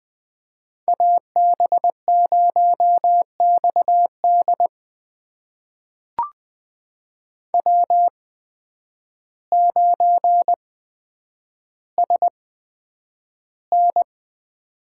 Morse code, heard as a keyed tone, about 20 wpm, spaced as for 9 wpm, 700 Hz. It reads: AB0XD W9SN
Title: USA Call Signs - 8x Character Spacing - 20wpm